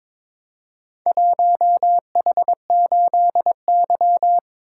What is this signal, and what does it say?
Morse code, heard as a keyed tone, about 22 wpm, 700 Hz. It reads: 1H8Y
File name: practice / 22wpm / Four Letter-Number Combinations - 22wpm